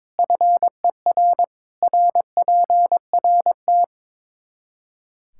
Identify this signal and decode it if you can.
Morse code, heard as a keyed tone, about 22 wpm, 700 Hz. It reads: FER RPRT
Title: K1USN SST QSO Practice 22wpm